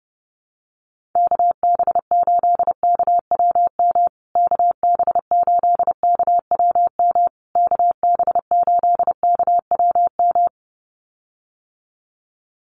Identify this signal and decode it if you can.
Morse code, heard as a keyed tone, about 30 wpm, 700 Hz. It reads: KB8KWM KB8KWM KB8KWM
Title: USA Call Signs Repeated 3x 30wpm